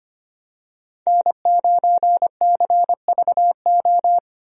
Morse code, heard as a keyed tone, about 25 wpm, 700 Hz. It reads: N9CVO
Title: USA Call Signs - 5 Characters - Repeated 3x - 25wpm